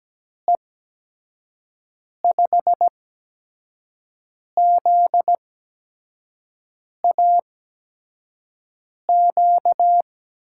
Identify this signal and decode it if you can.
Morse code, heard as a keyed tone, about 17 wpm, spaced as for 5 wpm, 700 Hz. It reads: E5ZAQ
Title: USA Call Signs - 6 Characters - 8x Character Spacing - 17wpm